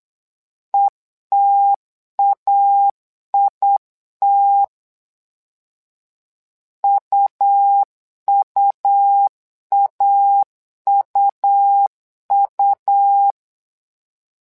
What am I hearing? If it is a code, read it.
Morse code, heard as a keyed tone, about 8 wpm, 800 Hz. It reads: ETAIT UUAUU
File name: LEZIONI / 2 UHDN / 2 U 8wpm 2028